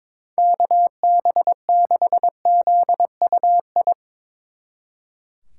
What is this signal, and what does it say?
Morse code, heard as a keyed tone, about 22 wpm, 700 Hz. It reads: KB6ZUI
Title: Speed Racing - USA Call Signs 22wpm